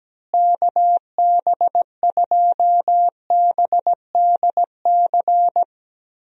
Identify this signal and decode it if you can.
Morse code, heard as a keyed tone, about 17 wpm, 700 Hz. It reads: KB2BDC
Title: USA Call Signs - 6 Characters - Repeated 2x - 17wpm